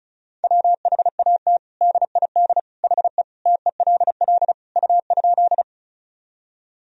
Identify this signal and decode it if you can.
Morse code, heard as a keyed tone, about 35 wpm, 700 Hz. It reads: WHAT DID HE TELL U?
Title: Sentences from Top 200 Words and Abbreviations 35wpm